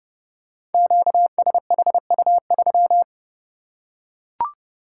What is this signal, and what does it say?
Morse code, heard as a keyed tone, about 31 wpm, 700 Hz. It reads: QSHU3
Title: Code Groups - 5 Letter:Numbers - 30wpm